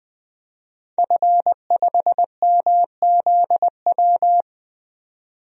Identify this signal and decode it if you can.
Morse code, heard as a keyed tone, about 21 wpm, 700 Hz. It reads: F5MZW
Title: DXCC-100 Call Signs - 5 Characters - 20wpm